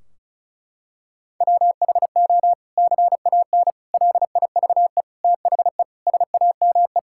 Morse code, heard as a keyed tone, about 34 wpm, 700 Hz. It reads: WHO CAN LIVE THE SAME
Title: Sentences from Top 1600 Words 35wpm